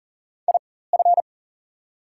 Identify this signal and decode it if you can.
Morse code, heard as a keyed tone, about 39 wpm, spaced as for 19 wpm, 700 Hz. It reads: IF
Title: 4x Character Spacing - Top 200 Words - 40wpm